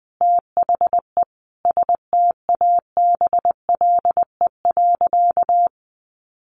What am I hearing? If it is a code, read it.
Morse code, heard as a keyed tone, about 20 wpm, 700 Hz. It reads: THE STABLE.
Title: Sentences from Top 1500 Words 20wpm